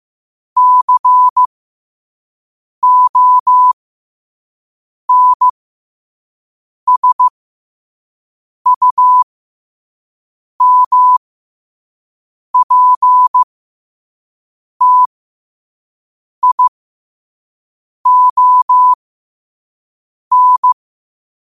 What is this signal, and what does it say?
Morse code, heard as a keyed tone, about 15 wpm, spaced as for 5 wpm, 1000 Hz. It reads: CONSUMPTION
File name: qotdfiles / 5wpm